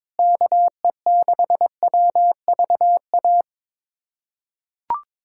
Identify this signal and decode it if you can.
Morse code, heard as a keyed tone, about 22 wpm, 700 Hz. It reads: KE6WVA
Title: USA Callsign DE Callsign - 6 Characters - 22wpm